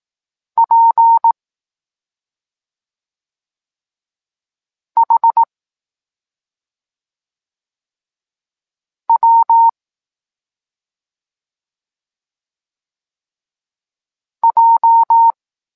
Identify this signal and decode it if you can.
Morse code, heard as a keyed tone, about 18 wpm, 920 Hz. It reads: P H W J
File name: Tekst-26 1